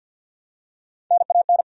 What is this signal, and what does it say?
Morse code, heard as a keyed tone, about 51 wpm, 700 Hz. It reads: NAN
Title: USA Common Names - Max 3 Letters - 50wpm